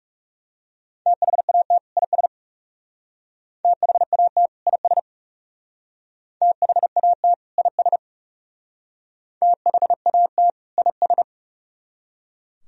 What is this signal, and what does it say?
Morse code, heard as a keyed tone, about 37 wpm, 700 Hz. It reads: THAT IS THAT IS THAT IS THAT IS
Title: Speed-Racing - Sets of 2 Words 30wpm